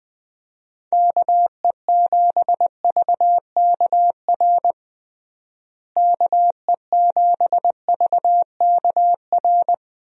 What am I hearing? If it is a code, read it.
Morse code, heard as a keyed tone, about 20 wpm, 700 Hz. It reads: KE7VKR KE7VKR
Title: USA Call Signs - 6 Characters - Repeated 3x - 20wpm